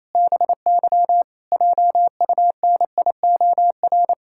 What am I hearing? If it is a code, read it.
Morse code, heard as a keyed tone, about 28 wpm, 700 Hz. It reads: BY JUNIOR
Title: Pangram Challenge v2 - 28wpm